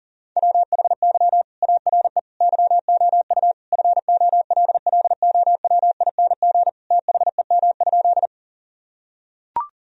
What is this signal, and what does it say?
Morse code, heard as a keyed tone, about 40 wpm, 700 Hz. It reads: WHY ARE YOU FOLLOWING THEM?
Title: Sentences from Top 1000 Words - Review Part 4 - 40wpm